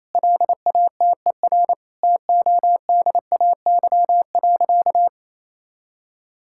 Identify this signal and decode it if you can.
Morse code, heard as a keyed tone, about 28 wpm, 700 Hz. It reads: LATER TODAY.
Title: Sentences from Top 1000 Words - Review Part 5 - 28wpm